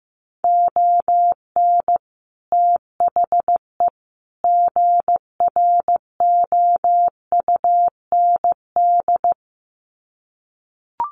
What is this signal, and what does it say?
Morse code, heard as a keyed tone, about 15 wpm, 700 Hz. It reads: ON THE GROUND
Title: Sets of 3 Words 15wpm